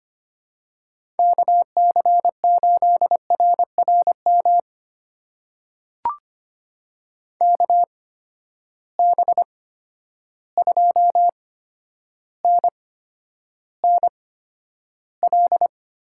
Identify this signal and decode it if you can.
Morse code, heard as a keyed tone, about 25 wpm, spaced as for 11 wpm, 700 Hz. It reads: KC8RRM KB2NNL
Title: USA Call Signs - 6 Characters - 8x Character Spacing - 25wpm